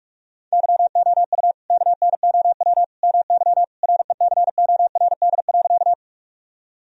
Morse code, heard as a keyed tone, about 46 wpm, 700 Hz. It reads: YOU KNOW MY RECORD.
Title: Sentences from Top 1000 Words - Review Part 3 - 45wpm